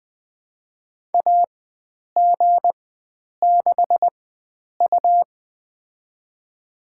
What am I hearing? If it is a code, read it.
Morse code, heard as a keyed tone, about 20 wpm, spaced as for 9 wpm, 700 Hz. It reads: AG6U
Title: USA Call Signs - 4 Characters - 4x Character Spacing - 20wpm